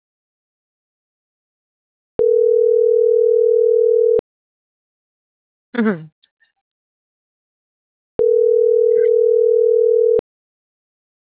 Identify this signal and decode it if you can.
Ringback tone (North America), 440 + 480 Hz.